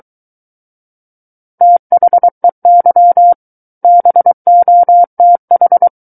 Morse code, heard as a keyed tone, about 23 wpm, 700 Hz. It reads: THEY BOTH
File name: Book One / 14-Chapter 14 - 23 WPM